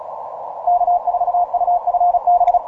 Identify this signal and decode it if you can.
Morse code, heard as a keyed tone, about 36 wpm, 710 Hz, with radio noise. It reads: K4UFB
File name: simple-file-list / CE0Y EA5KA 12MCW